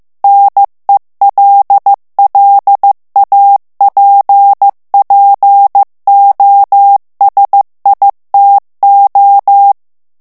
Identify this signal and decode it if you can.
Morse code, heard as a keyed tone, about 15 wpm, 800 Hz. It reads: NELLAPPOSITO
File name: morse mp3 / ITALIEN texte clair 2 vitesse 14 mots minute